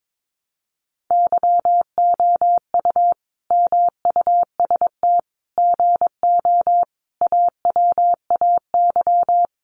Morse code, heard as a keyed tone, about 22 wpm, 700 Hz. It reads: YOU MUST GO AWAY
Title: Sentences from Top 1100 Words 22wpm